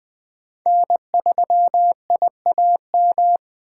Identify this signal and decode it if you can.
Morse code, heard as a keyed tone, about 20 wpm, 700 Hz. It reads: N3IAM
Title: Speed Racing - USA 1x3 Call Signs 17wpm